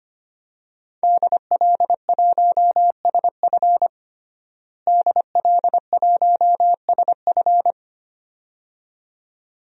Morse code, heard as a keyed tone, about 25 wpm, 700 Hz. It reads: DL1SF DL1SF
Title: DXCC-100 Call Signs - 5 Characters - Repeated 2x - 25wpm